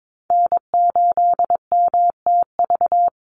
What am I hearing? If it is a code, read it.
Morse code, heard as a keyed tone, about 22 wpm, 700 Hz. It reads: N8MTV
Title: Call Signs Repeated 3x - 22wpm